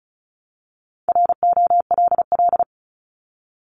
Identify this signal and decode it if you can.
Morse code, heard as a keyed tone, about 36 wpm, 700 Hz. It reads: ROLL
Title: Multi-letter Phonemes 35wpm